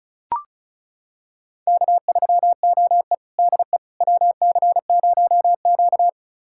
Morse code, heard as a keyed tone, about 36 wpm, 700 Hz. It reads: K3OE DE WC0Q
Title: USA Callsign DE Callsign - 4 Characters - 35wpm